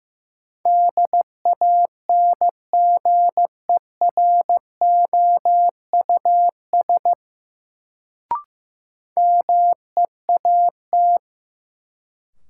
Morse code, heard as a keyed tone, about 15 wpm, 700 Hz. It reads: DANGEROUS MEAT
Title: Sentences from Top 1200 Words - New Word Drill Alt - 15wpm